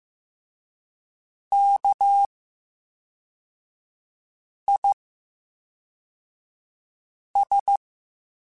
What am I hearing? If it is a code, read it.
Morse code, heard as a keyed tone, about 15 wpm, 800 Hz. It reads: K I S
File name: Special INGs / Special INGs-mp3s / VerySlowToFaster Ing @10